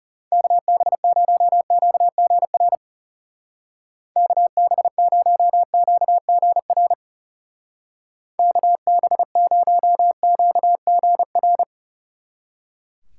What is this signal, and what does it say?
Morse code, heard as a keyed tone, about 35 wpm, 700 Hz. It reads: KB0QGR KB0QGR KB0QGR
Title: Speed Racing - USA Call Signs 30wpm